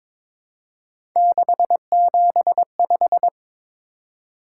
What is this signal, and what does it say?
Morse code, heard as a keyed tone, about 22 wpm, 700 Hz. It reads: 675
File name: practice / 22wpm / Three Letter-Number Combinations - 22wpm